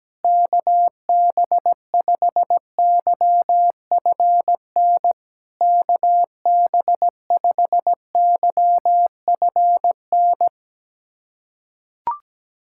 Morse code, heard as a keyed tone, about 17 wpm, 700 Hz. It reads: KB5YFN KB5YFN
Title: SOTA QSO Practice 17wpm